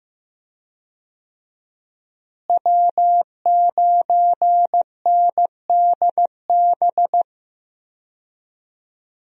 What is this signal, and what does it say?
Morse code, heard as a keyed tone, about 15 wpm, 700 Hz. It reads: W9NDB
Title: USA Call Signs - 5 Characters - Repeated 3x - 15wpm